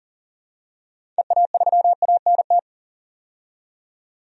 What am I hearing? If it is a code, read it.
Morse code, heard as a keyed tone, about 40 wpm, 700 Hz. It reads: EA3ANT
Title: DXCC-100 Call Signs - 6 Characters - 40wpm